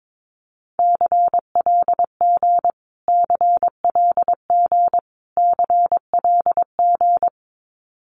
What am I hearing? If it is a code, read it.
Morse code, heard as a keyed tone, about 22 wpm, 700 Hz. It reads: CLG CLG CLG
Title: Abbreviations Repeated 3x - 22wpm